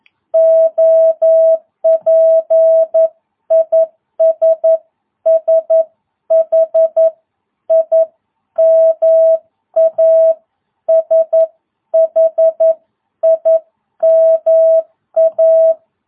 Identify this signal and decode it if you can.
Morse code, heard as a keyed tone, about 11 wpm, spaced as for 10 wpm, 650 Hz. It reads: OP IS SHIMA SHIMA